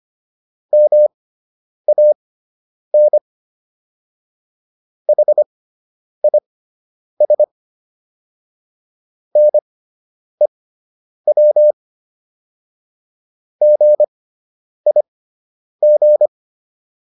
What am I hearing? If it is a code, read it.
Morse code, heard as a keyed tone, about 25 wpm, spaced as for 9 wpm, 600 Hz. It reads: MAN HIS NEW GIG